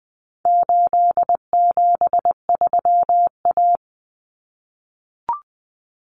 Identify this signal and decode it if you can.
Morse code, heard as a keyed tone, about 20 wpm, 700 Hz. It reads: 873A
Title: Common Cut Numbers - 4-Digit 20wpm